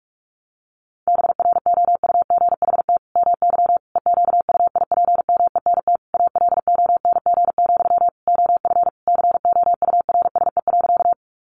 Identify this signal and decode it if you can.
Morse code, heard as a keyed tone, about 44 wpm, 700 Hz. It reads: BROUGHT MY EQUIPMENT ALONG, OF COURSE.
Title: Sentences from Top 700 Words 45wpm